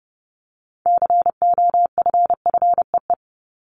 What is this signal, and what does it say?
Morse code, heard as a keyed tone, about 30 wpm, 700 Hz. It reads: COFFEE
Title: Top 1751-2000 Words 30wpm